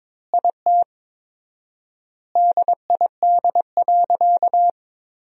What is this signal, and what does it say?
Morse code, heard as a keyed tone, about 22 wpm, 700 Hz. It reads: IT DID.
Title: 4x Word-Spacing - Sentences from Top 500 Words - 22wpm